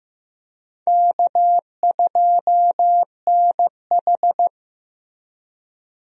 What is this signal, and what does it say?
Morse code, heard as a keyed tone, about 15 wpm, 700 Hz. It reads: K2NH
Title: USA Call Signs - 4 Characters - Repeated 3x - 15wpm